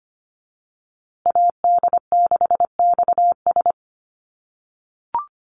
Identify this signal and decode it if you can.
Morse code, heard as a keyed tone, about 25 wpm, 700 Hz. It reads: AD6XS
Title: USA Call Signs Repeated 2x 25wpm